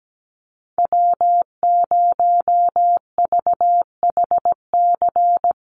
Morse code, heard as a keyed tone, about 17 wpm, 700 Hz. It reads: W0VHC
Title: USA Call Signs Repeated 3x 17wpm